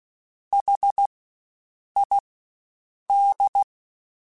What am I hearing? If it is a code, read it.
Morse code, heard as a keyed tone, about 16 wpm, spaced as for 7 wpm, 800 Hz. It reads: HID
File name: Spaced Alphas / Spaced Alphas mp3s / Hs sp @15